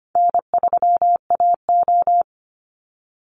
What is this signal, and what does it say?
Morse code, heard as a keyed tone, about 25 wpm, 700 Hz. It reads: N3AO
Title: Call Signs - 25wpm